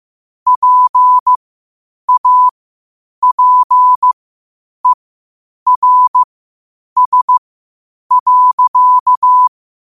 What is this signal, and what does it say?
Morse code, heard as a keyed tone, about 15 wpm, spaced as for 8 wpm, 1000 Hz. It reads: PAPERS.
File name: qotdfiles / 8wpm